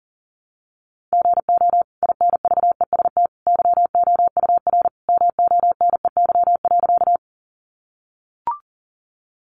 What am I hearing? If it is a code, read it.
Morse code, heard as a keyed tone, about 40 wpm, 700 Hz. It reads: GO INVEST YOUR MONEY.
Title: Sentences from Top 1100 Words 40wpm